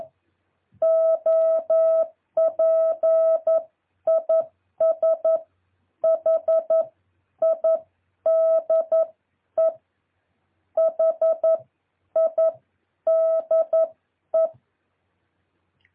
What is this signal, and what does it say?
Morse code, heard as a keyed tone, about 11 wpm, spaced as for 9 wpm, 650 Hz. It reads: OP IS HIDE HIDE
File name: jh8cbh / 2023cw